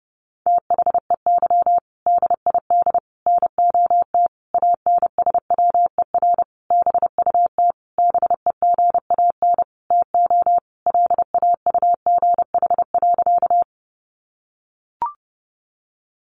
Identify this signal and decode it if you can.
Morse code, heard as a keyed tone, about 30 wpm, 700 Hz. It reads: THEY DID NOT ANSWER BUT BEGAN TO LAUGH.
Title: Sentences from Top 600 Words 30wpm